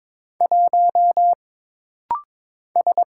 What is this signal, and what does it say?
Morse code, heard as a keyed tone, about 22 wpm, 700 Hz. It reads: 1 S
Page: This rapid-fire timing is geared to push your ICR (Instant Character Recognition) skills.